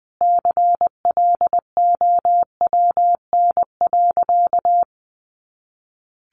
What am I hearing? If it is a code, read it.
Morse code, heard as a keyed tone, about 20 wpm, 700 Hz. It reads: CLOWN.
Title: Easy Phrases 20wpm